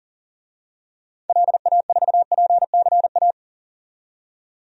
Morse code, heard as a keyed tone, about 37 wpm, 700 Hz. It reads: LA4PCA
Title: DXCC-100 Call Signs - 6 Characters - 40wpm